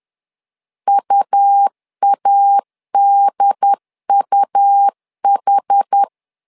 Morse code, heard as a keyed tone, about 10 wpm, 800 Hz. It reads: UADUH
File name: LEZIONI / 2 UHDN / 2 UHD 10wpm 1826